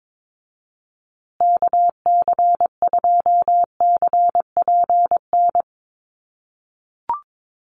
Morse code, heard as a keyed tone, about 22 wpm, 700 Hz. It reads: KC2CPN
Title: USA Call Signs Repeated 3x 22wpm